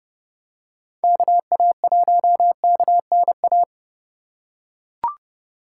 Morse code, heard as a keyed tone, about 30 wpm, 700 Hz. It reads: KA1KNA
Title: USA Call Signs - 6 Characters - 3x Character Spacing - 30wpm